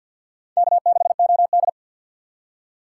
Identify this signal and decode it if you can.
Morse code, heard as a keyed tone, about 47 wpm, 700 Hz. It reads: K6OD
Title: USA Call Signs - 4x Character Spacing - 50wpm